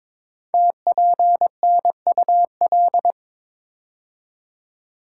Morse code, heard as a keyed tone, about 23 wpm, 700 Hz. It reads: TPNUL
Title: Code Groups - 5 Letters - 22wpm